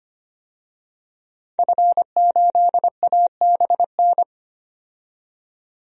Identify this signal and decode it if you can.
Morse code, heard as a keyed tone, about 25 wpm, 700 Hz. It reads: F8ABN
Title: DXCC-100 Call Signs - 5 Characters - 25wpm